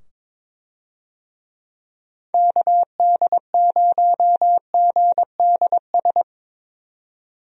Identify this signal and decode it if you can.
Morse code, heard as a keyed tone, about 22 wpm, 700 Hz. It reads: KD0GDS
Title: USA Call Signs - 6 Characters - 22wpm